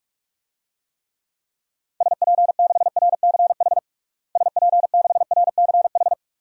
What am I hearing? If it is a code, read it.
Morse code, heard as a keyed tone, about 41 wpm, 700 Hz. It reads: SP6RCH SP6RCH
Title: DXCC-100 Call Signs - 6 Characters - Repeated 3x - 45wpm